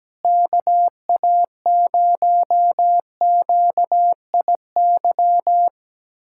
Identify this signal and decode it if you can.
Morse code, heard as a keyed tone, about 17 wpm, 700 Hz. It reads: KA0QIY
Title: USA Callsign DE Callsign - 6 Characters - 17wpm